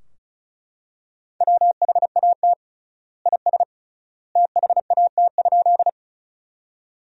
Morse code, heard as a keyed tone, about 36 wpm, 700 Hz. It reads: WHAT IS THAT?
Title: 3x Word-Spacing - Sentences from Top 100 Words - 35wpm